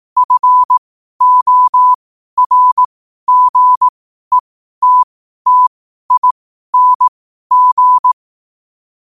Morse code, heard as a keyed tone, about 18 wpm, spaced as for 12 wpm, 1000 Hz. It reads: FORGETTING
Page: 12-wpm-morse-code-quotes